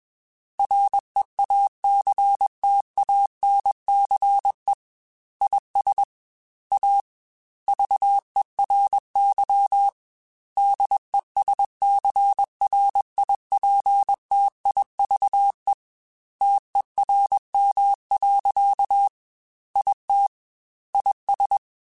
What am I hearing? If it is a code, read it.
Morse code, heard as a keyed tone, about 21 wpm, 800 Hz. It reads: REACTANCE IS A VERY DESCRIPTIVE TERM. IT IS
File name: Technical Subjects / Technical Subjects-mp3s / Reactance @20